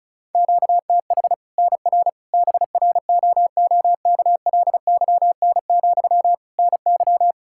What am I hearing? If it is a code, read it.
Morse code, heard as a keyed tone, about 36 wpm, 700 Hz. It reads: QTH NR BROOKLYN, NY
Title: Practice QSO - Exchange 1 - Beginning - 35wpm